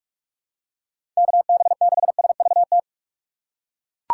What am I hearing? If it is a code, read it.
Morse code, heard as a keyed tone, about 41 wpm, 700 Hz. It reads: KB6SVT
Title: USA Call Signs - 6 Characters - 4x Character Spacing - 45wpm